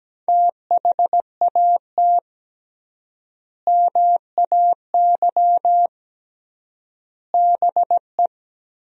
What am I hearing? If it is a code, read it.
Morse code, heard as a keyed tone, about 17 wpm, 700 Hz. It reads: THAT MAY BE
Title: 3x Word-Spacing - Sentences from Top 400 Words - 17wpm